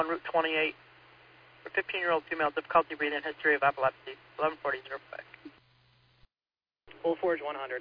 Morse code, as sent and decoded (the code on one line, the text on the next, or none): none